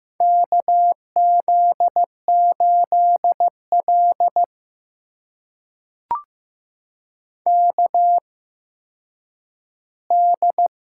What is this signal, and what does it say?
Morse code, heard as a keyed tone, about 15 wpm, spaced as for 9 wpm, 700 Hz. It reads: KZ8L KD
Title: USA Call Signs - 8x Character Spacing - 15wpm